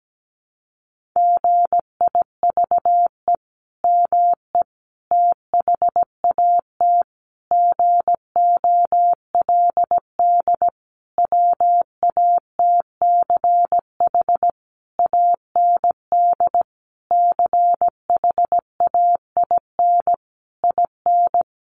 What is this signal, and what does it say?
Morse code, heard as a keyed tone, about 17 wpm, 700 Hz. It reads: GIVE ME THAT GOLD WATCH AND CHAIN IN
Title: Sentences from Top 1500 Words 17wpm